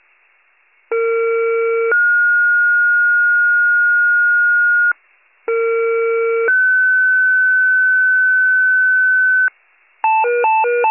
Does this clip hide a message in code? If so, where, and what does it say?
Yes: two-tone page from 0.9 to 9.5 s, heard as a one-second tone, then a three-second tone, about 460 then 1430 Hz, about 460 then 1500 Hz.